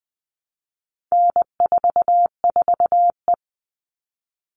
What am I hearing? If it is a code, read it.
Morse code, heard as a keyed tone, about 20 wpm, 700 Hz. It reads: N44E
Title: All Cut Numbers - 4-Digit 20wpm